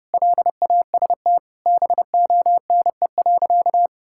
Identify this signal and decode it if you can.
Morse code, heard as a keyed tone, about 32 wpm, 700 Hz. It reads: LAST BONE.
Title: Sentences from Top 1600 Words 30wpm